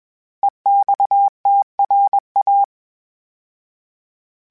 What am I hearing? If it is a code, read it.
Morse code, heard as a keyed tone, about 21 wpm, 800 Hz. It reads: EXTRA
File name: fromSMs / FromSMs mp3s / Classes@20